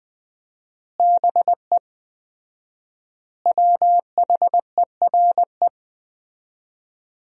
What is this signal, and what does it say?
Morse code, heard as a keyed tone, about 21 wpm, 700 Hz. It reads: BE WHERE
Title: 4x Word-Spacing - Sentences from Top 200 Words - 20wpm